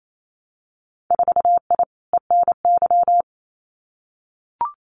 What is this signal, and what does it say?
Morse code, heard as a keyed tone, about 28 wpm, 700 Hz. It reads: 4I ENY
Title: Winter Field Day Exchanges Encore 28wpm